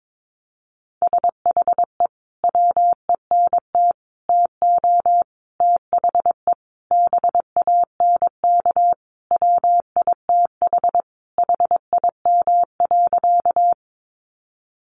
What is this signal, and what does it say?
Morse code, heard as a keyed tone, about 22 wpm, 700 Hz. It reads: SHE WENT TO THE BANK WITH HIM.
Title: Sentences from Top 900 Words 22wpm